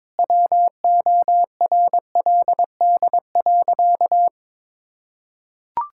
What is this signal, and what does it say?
Morse code, heard as a keyed tone, about 21 wpm, 700 Hz. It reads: WORLD.
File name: Sentences with 7-Word Phrases - 22wpm